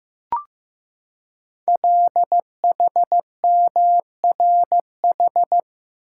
Morse code, heard as a keyed tone, about 15 wpm, 700 Hz. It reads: LHMRH
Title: Code Groups - 5 Letters - 15wpm